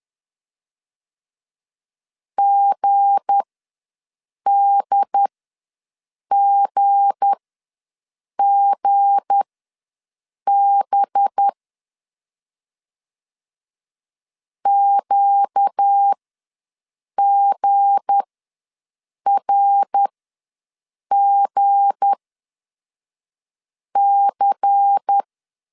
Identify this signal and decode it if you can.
Morse code, heard as a keyed tone, about 10 wpm, spaced as for 6 wpm, 800 Hz. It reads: GDGGB QGRGC
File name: LEZIONI / 7 GF9Z / 7 G ws 10wpm 3843